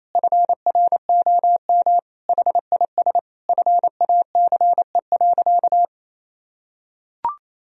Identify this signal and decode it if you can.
Morse code, heard as a keyed tone, about 27 wpm, 700 Hz. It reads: FROM HIS FACE.
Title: Sentences from Top 1000 Words - Review Part 2 - 28wpm